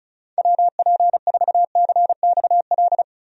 Morse code, heard as a keyed tone, about 36 wpm, 700 Hz. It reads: WP4CXL
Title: USA Call Signs - 6 Characters - 35wpm